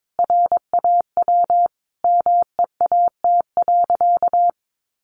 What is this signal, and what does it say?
Morse code, heard as a keyed tone, about 22 wpm, 700 Hz. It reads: RAW MEAT.
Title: Sentences from Top 1500 Words 22wpm